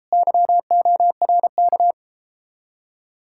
Morse code, heard as a keyed tone, about 37 wpm, 700 Hz. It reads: YORK